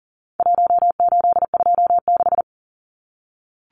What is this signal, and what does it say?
Morse code, heard as a keyed tone, about 40 wpm, 700 Hz. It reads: 1826
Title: Numbers - Contest Style - 4-Digit 40wpm